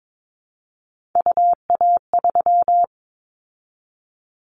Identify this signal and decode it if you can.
Morse code, heard as a keyed tone, about 22 wpm, 700 Hz. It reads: UA3
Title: Callsign Prefixes for Top 25 DXCC - 22wpm